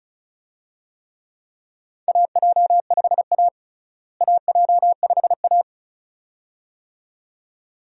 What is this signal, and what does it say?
Morse code, heard as a keyed tone, about 34 wpm, 700 Hz. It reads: AJ5A AJ5A
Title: USA Call Signs - 4 Characters - Repeated 2x - 35wpm